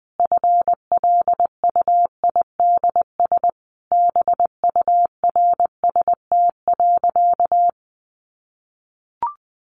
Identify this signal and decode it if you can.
Morse code, heard as a keyed tone, about 20 wpm, 700 Hz. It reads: FLUIDS BURST.
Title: A Sound Of Thunder 20wpm